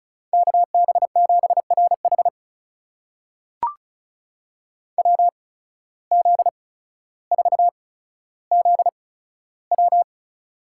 Morse code, heard as a keyed tone, about 36 wpm, spaced as for 15 wpm, 700 Hz. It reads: KB7RH WZ4ZW
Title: USA Call Signs - 5 Characters - 8x Character Spacing - 35wpm